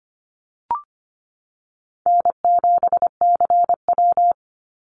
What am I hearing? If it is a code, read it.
Morse code, heard as a keyed tone, about 25 wpm, 700 Hz. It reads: N7CW
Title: NA SOTA Activator Callsigns 25wpm